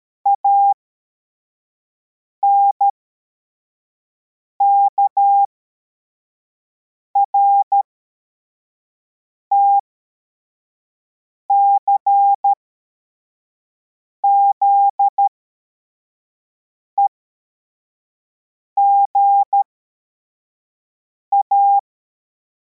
Morse code, heard as a keyed tone, about 13 wpm, spaced as for 4 wpm, 800 Hz. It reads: ANKRT CZEGA
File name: formation / cw / Text42